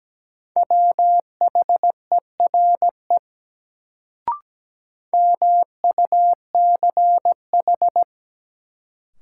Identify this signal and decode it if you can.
Morse code, heard as a keyed tone, about 17 wpm, 700 Hz. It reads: WHERE MUCH
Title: Sets of 2 Words - New Word Drill - 17wpm